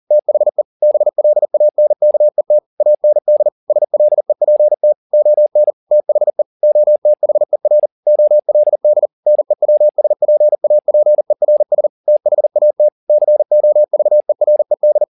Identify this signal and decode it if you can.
Morse code, heard as a keyed tone, about 40 wpm, 600 Hz. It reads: THE BLANKET AND SLEPT ON THE OTHER OLD NEWSPAPERS THAT COVERED